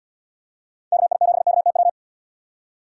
Morse code, heard as a keyed tone, about 78 wpm, 700 Hz. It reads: KI7KTW